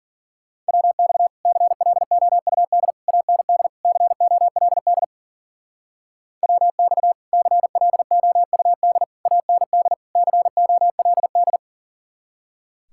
Morse code, heard as a keyed tone, about 50 wpm, 700 Hz. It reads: WX CLOUD AND COLD WX CLOUD AND COLD